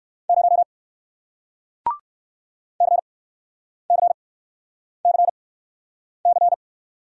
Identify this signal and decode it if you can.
Morse code, heard as a keyed tone, about 61 wpm, spaced as for 9 wpm, 700 Hz. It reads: 0 CCCC